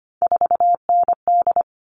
Morse code, heard as a keyed tone, about 25 wpm, 700 Hz. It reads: 4ND
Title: All Cut Numbers - 3-Digit 25wpm